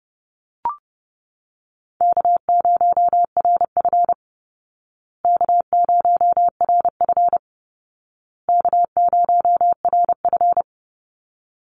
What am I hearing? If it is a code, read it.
Morse code, heard as a keyed tone, about 30 wpm, 700 Hz. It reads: K0RF K0RF K0RF
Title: Easy Call Signs Repeated 3x - 30wpm